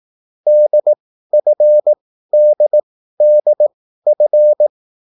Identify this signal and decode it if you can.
Morse code, heard as a keyed tone, about 18 wpm, spaced as for 13 wpm, 600 Hz. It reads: DFDDF